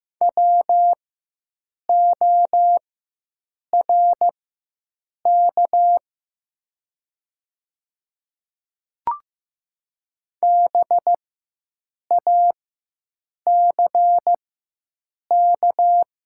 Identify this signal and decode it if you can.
Morse code, heard as a keyed tone, about 15 wpm, spaced as for 7 wpm, 700 Hz. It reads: WORK BACK
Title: 4x Character Spacing - Top 200 Words - 15wpm